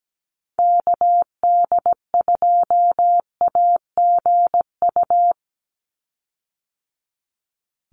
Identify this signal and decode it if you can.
Morse code, heard as a keyed tone, about 17 wpm, 700 Hz. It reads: KD2AGU
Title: USA Call Signs 17wpm